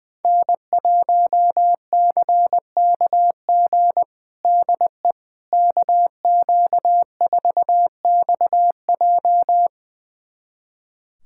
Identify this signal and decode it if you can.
Morse code, heard as a keyed tone, about 21 wpm, 700 Hz. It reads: N1CKG DE KQ4XJ
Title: USA Callsign DE Callsign - 5 Characters - 20wpm